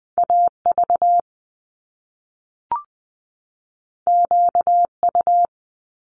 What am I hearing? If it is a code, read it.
Morse code, heard as a keyed tone, about 20 wpm, 700 Hz. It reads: AV QU
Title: Top Two Letter Combinations in Words 20wpm